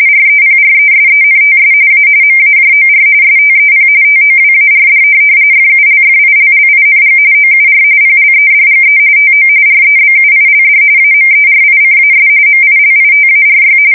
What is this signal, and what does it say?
RTTY, about 45.45 baud. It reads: REPRESENTS A MAJOR STEP TOWARD FULFILLING OUR PLEDGE TO REBUILD AMERICAS MILITARY
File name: fsk170 / USS-New-Jersey